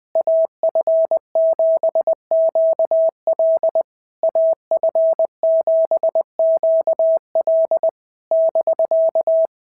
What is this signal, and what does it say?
Morse code, heard as a keyed tone, about 20 wpm, 650 Hz. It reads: AF7QL AF7QL <BK>
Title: SOTA QSO Practice 20wpm